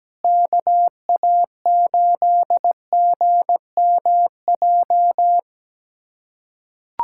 Morse code, heard as a keyed tone, about 17 wpm, 700 Hz. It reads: KA8GMJ
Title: USA Callsign DE Callsign - 6 Characters - 17wpm